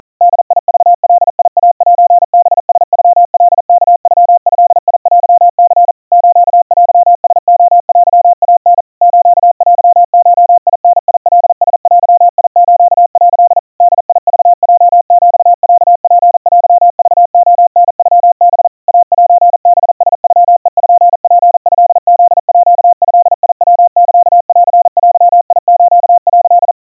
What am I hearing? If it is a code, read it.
Morse code (Japanese Wabun code), heard as a keyed tone, about 40 wpm, 700 Hz. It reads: ボクガイセハラノカワノドテニ ステラレテイタ ステゴダカラデス」 ボクヲヒロツテクレタノハ イセハラノペツトフードヤサンデス」
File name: kawa / Morse sample / kamyu 150 cpm 700 Hz Crispy